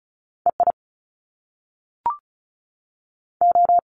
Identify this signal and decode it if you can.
Morse code, heard as a keyed tone, about 36 wpm, 700 Hz. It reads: E I O
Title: Top Two Letter Combinations in Words 35wpm